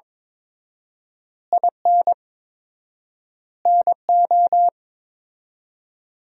Morse code, heard as a keyed tone, about 22 wpm, 700 Hz. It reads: IN NO
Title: 4x Word-Spacing - Sentences from Top 400 Words - 22wpm